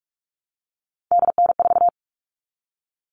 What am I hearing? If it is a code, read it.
Morse code, heard as a keyed tone, about 44 wpm, 700 Hz. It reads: DN4
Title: All Cut Numbers - 3-Digit 45wpm